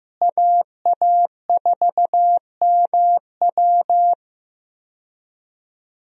Morse code, heard as a keyed tone, about 16 wpm, 700 Hz. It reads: AA4MW
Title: USA Call Signs - 5 Characters - Repeated 2x - 15wpm